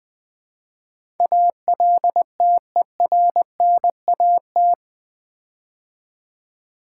Morse code, heard as a keyed tone, about 20 wpm, 700 Hz. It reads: ALTERNAT
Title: Build A Word - 4 Letters At A Time - 11-Letter Words - 20wpm